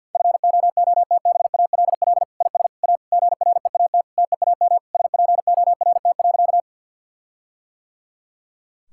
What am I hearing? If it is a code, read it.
Morse code, heard as a keyed tone, about 52 wpm, 700 Hz. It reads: FOOTBALL IS A GREAT TEAM SPORT.
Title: Sentences from Top 1600 Words 50wpm